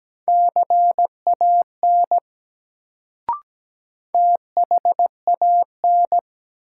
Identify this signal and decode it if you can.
Morse code, heard as a keyed tone, about 17 wpm, 700 Hz. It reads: CAN THAN
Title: Sentences from Top 100 Words - New Word Drill - 17wpm